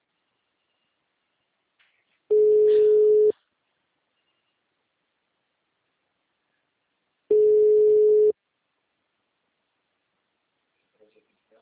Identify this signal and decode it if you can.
Ringback tone (Europe, 425 Hz), with speech.
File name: out- -2006- - - .33116